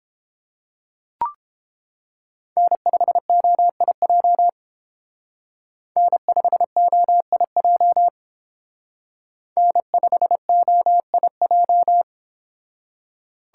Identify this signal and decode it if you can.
Morse code, heard as a keyed tone, about 28 wpm, 700 Hz. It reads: N5OIJ N5OIJ N5OIJ